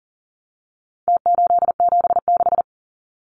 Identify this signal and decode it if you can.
Morse code, heard as a keyed tone, about 40 wpm, 700 Hz. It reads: T876
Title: Numbers - Contest Style - 4-Digit 40wpm